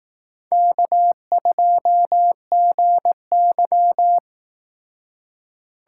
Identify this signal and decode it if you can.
Morse code, heard as a keyed tone, about 18 wpm, 700 Hz. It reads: K2GY
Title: Speed Racing - USA 1x2 Call Signs 15wpm